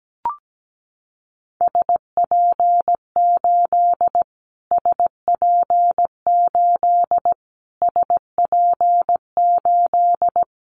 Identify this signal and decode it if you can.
Morse code, heard as a keyed tone, about 17 wpm, 700 Hz. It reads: SP8 SP8 SP8
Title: Callsign Prefixes for Top 25 DXCC - Repeated 3x - 17wpm